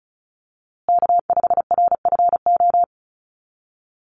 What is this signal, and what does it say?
Morse code, heard as a keyed tone, about 34 wpm, 700 Hz. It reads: K5RFO
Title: USA Call Signs Repeated 2x 35wpm